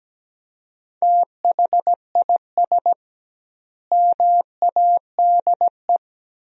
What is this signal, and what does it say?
Morse code, heard as a keyed tone, about 17 wpm, 700 Hz. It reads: THIS MADE
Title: 2x Word-Spacing - Sentences from Top 500 Words - 17wpm